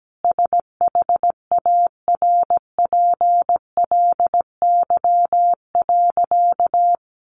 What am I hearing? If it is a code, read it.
Morse code, heard as a keyed tone, about 17 wpm, 700 Hz. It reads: SHARPLY.
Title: A Sound Of Thunder 17wpm